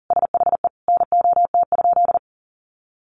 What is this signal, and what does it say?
Morse code, heard as a keyed tone, about 40 wpm, 700 Hz. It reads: SHE NOT?
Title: Sentences from Top 1300 Words 40wpm